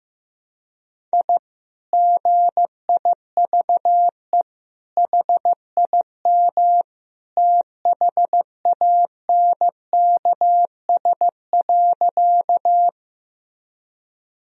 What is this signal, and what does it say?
Morse code, heard as a keyed tone, about 15 wpm, 700 Hz. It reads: I GIVE HIM THANKS.
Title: Sentences from Top 1000 Words - Review Part 3 - 15wpm